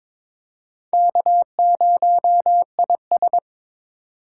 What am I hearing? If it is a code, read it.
Morse code, heard as a keyed tone, about 22 wpm, 700 Hz. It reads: K0IS
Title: USA Call Signs - 4 Characters - 3x Character Spacing - 22wpm